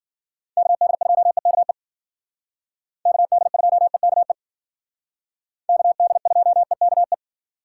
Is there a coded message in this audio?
Yes: Morse code from 0.6 to 7.1 s, heard as a keyed tone, about 50 wpm, 700 Hz. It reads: KD2EKE KD2EKE KD2EKE